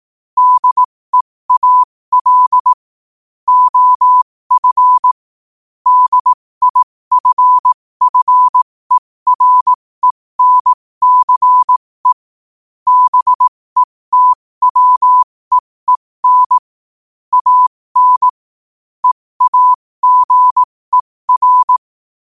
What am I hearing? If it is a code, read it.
Morse code, heard as a keyed tone, about 18 wpm, spaced as for 15 wpm, 1000 Hz. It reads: DEAL OF DIFFERENCE BETWEEN AN EAGER
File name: qotdfiles / 15wpm